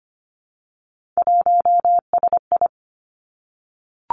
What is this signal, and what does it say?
Morse code, heard as a keyed tone, about 25 wpm, 700 Hz. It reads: 1SI
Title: Three Challenging Letter-Number Combinations 25wpm